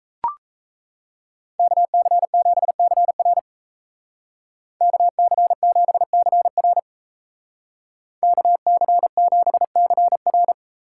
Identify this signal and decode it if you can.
Morse code, heard as a keyed tone, about 33 wpm, 700 Hz. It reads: KC7CR KC7CR KC7CR
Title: Speed Racing - USA 2x2 Call Signs 28wpm